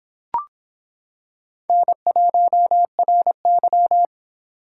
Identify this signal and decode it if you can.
Morse code, heard as a keyed tone, about 25 wpm, 700 Hz. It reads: N1RY